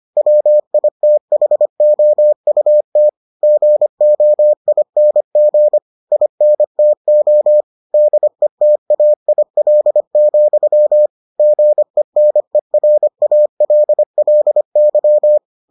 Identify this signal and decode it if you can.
Morse code, heard as a keyed tone, about 25 wpm, 600 Hz. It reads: WITHOUT GOING INTO DETAIL, GENERALLY